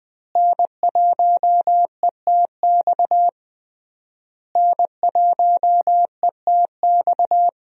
Morse code, heard as a keyed tone, about 21 wpm, 700 Hz. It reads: N1ETX N1ETX
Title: USA Call Signs - 5 Characters - Repeated 3x - 20wpm